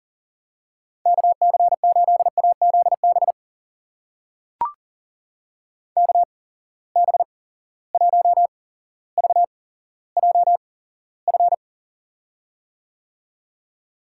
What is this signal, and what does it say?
Morse code, heard as a keyed tone, about 38 wpm, spaced as for 17 wpm, 700 Hz. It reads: KC8AZB KB1VJF
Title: USA Call Signs - 6 Characters - 8x Character Spacing - 40wpm